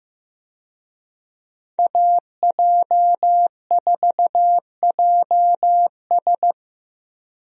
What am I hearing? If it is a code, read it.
Morse code, heard as a keyed tone, about 15 wpm, 700 Hz. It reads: AJ4JS
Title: USA Call Signs - 5 Characters - 15wpm